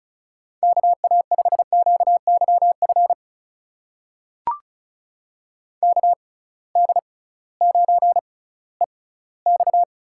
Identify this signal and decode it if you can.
Morse code, heard as a keyed tone, about 34 wpm, spaced as for 19 wpm, 700 Hz. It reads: KA5QYF KD9EX
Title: USA Call Signs - 6 Characters - 6x Character Spacing - 35wpm